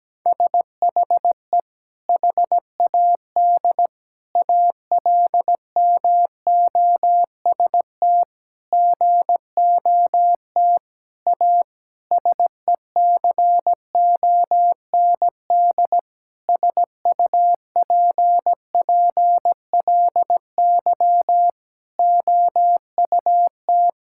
Morse code, heard as a keyed tone, about 17 wpm, 700 Hz. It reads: SHE HAD ALMOST GOT A SECOND SUPPLY OUT
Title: Sentences from Top 1000 Words - Review Part 5 - 17wpm